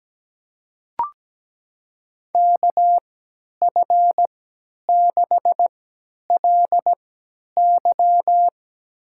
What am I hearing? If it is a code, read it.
Morse code, heard as a keyed tone, about 17 wpm, spaced as for 10 wpm, 700 Hz. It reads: KF6LY
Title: USA Call Signs - 5 Characters - 3x Character Spacing - 17wpm